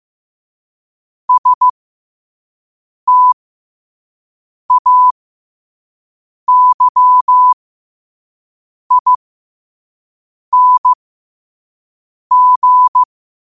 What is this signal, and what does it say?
Morse code, heard as a keyed tone, about 15 wpm, spaced as for 5 wpm, 1000 Hz. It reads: STAYING
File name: qotdfiles / 5wpm